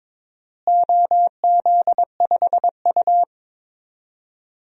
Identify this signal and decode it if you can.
Morse code, heard as a keyed tone, about 22 wpm, 700 Hz. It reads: OZ5U
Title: DXCC-100 Call Signs - 4 Characters - Repeated 3x - 22wpm